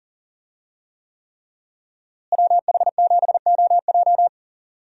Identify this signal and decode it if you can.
Morse code, heard as a keyed tone, about 40 wpm, 700 Hz. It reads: WH7OJ
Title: USA Call Signs - 5 Characters - Repeated 3x - 40wpm